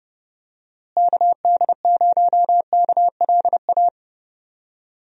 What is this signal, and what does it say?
Morse code, heard as a keyed tone, about 30 wpm, 700 Hz. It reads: KD0KLA
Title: USA Call Signs - 6 Characters - Repeated 3x - 30wpm